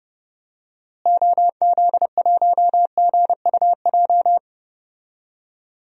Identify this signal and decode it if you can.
Morse code, heard as a keyed tone, about 30 wpm, 700 Hz. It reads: OZ1GUJ
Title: DXCC-100 Call Signs - 6 Characters - Repeated 3x - 30wpm